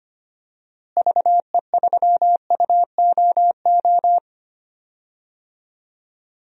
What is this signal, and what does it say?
Morse code, heard as a keyed tone, about 25 wpm, 700 Hz. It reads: VE3UOO
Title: DXCC-100 Call Signs - 6 Characters - Repeated 3x - 25wpm